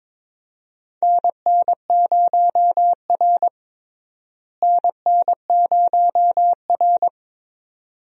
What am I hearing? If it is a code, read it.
Morse code, heard as a keyed tone, about 22 wpm, 700 Hz. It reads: NN0R NN0R
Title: USA Call Signs - 4 Characters - Repeated 2x - 22wpm